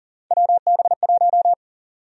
Morse code, heard as a keyed tone, about 40 wpm, 700 Hz. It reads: WB1
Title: USA Call Sign Prefixes - Repeated 2x - 40wpm